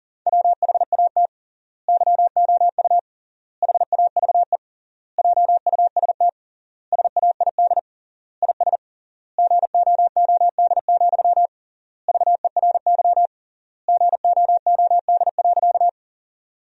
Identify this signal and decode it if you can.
Morse code, heard as a keyed tone, about 40 wpm, 700 Hz. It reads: WHAT YOU HAVE JUST SAID IS GOOD, VERY GOOD.
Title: 3x Word-Spacing - Sentences from Top 200 Words - 40wpm